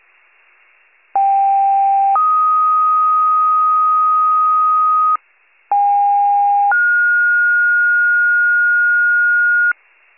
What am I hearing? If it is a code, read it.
Two-tone page, heard as a one-second tone, then a three-second tone, about 790 then 1250 Hz, about 810 then 1430 Hz. It is heard from 1.1 to 9.7 s.
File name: pager / 38 FIRE / wcalert